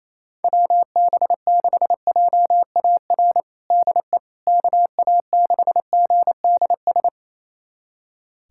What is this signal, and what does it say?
Morse code, heard as a keyed tone, about 28 wpm, 700 Hz. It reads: WB6JAR DE KA6GDS
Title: USA Callsign DE Callsign - 6 Characters - 28wpm